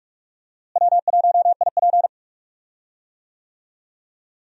Morse code, heard as a keyed tone, about 42 wpm, 700 Hz. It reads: W1IP
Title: USA Call Signs - 4 Characters - Repeated 2x - 45wpm